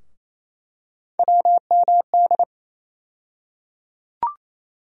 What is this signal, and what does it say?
Morse code, heard as a keyed tone, about 27 wpm, 700 Hz. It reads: WMD
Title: USA Call Sign Postfixes - 3 Letters 28wpm